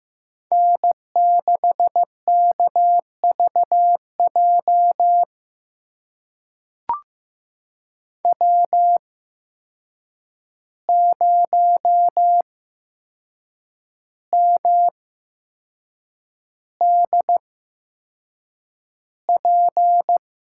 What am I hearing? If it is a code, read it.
Morse code, heard as a keyed tone, about 15 wpm, spaced as for 6 wpm, 700 Hz. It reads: N6KVJ W0MDP
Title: USA Call Signs - 5 Characters - 8x Character Spacing - 15wpm